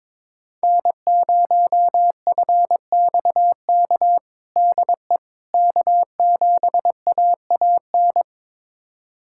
Morse code, heard as a keyed tone, about 22 wpm, 700 Hz. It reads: N0FXK DE K7AAN
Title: USA Callsign DE Callsign - 5 Characters - 22wpm